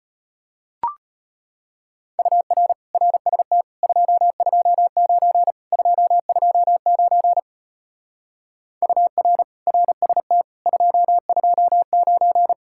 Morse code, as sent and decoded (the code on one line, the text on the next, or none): ..- .-. / .-. ... - / ..--- ..--- ----. / ..--- ..--- ----. / ..- .-. / .-. ... - / ..--- ..--- ----.
UR RST 229 229 UR RST 229